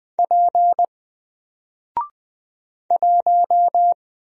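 Morse code, heard as a keyed tone, about 20 wpm, 700 Hz. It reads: P 1
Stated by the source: Single letter and number Morse drill at 20 wpm, from N4LCD.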